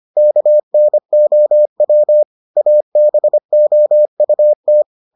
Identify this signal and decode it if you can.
Morse code, heard as a keyed tone, about 25 wpm, 600 Hz. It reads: KNOW ABOUT